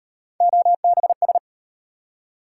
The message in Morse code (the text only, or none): OBS